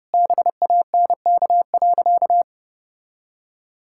Morse code, heard as a keyed tone, about 32 wpm, 700 Hz. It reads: BANK.
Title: Sentences from Top 1000 Words - Review Part 6 - 30wpm